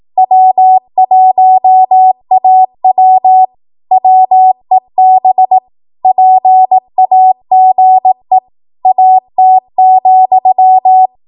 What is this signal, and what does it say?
Morse code, heard as a keyed tone, about 18 wpm, 750 Hz. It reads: W1AW WEB PAGE AT,
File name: Archive / 18 WPM / 18WPM